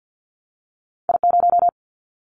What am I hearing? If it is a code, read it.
Morse code, heard as a keyed tone, about 50 wpm, 700 Hz. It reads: I0
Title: Callsign Prefixes for Top 50 DXCC - 50wpm